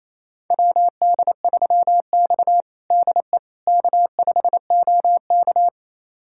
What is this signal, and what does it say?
Morse code, heard as a keyed tone, about 28 wpm, 700 Hz. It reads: WD3X DE K5OK
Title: USA Callsign DE Callsign - 4 Characters - 28wpm